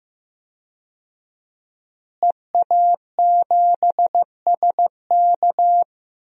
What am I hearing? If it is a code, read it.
Morse code, heard as a keyed tone, about 15 wpm, 700 Hz. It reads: EA7SK
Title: DXCC-100 Call Signs - 5 Characters - Repeated 3x - 15wpm